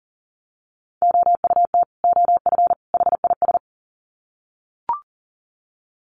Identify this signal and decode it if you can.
Morse code, heard as a keyed tone, about 38 wpm, 700 Hz. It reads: OUT OF HIS
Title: Sets of 3 Words 40wpm